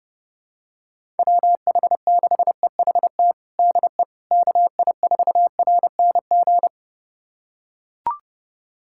Morse code, heard as a keyed tone, about 30 wpm, 700 Hz. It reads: WH6EHT DE KI4RNG
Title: USA Callsign DE Callsign 30wpm